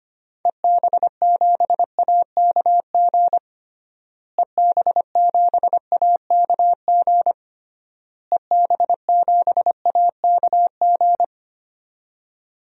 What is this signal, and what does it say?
Morse code, heard as a keyed tone, about 25 wpm, 700 Hz. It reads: EB7AKG EB7AKG EB7AKG
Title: DXCC-100 Call Signs - 6 Characters - Repeated 3x - 25wpm